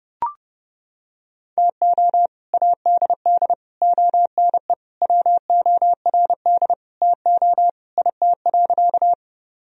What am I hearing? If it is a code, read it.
Morse code, heard as a keyed tone, about 30 wpm, 700 Hz. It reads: TO ADD ONE WORD TO IT.
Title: Sentences from Top 1000 Words - Review Part 1 - 30wpm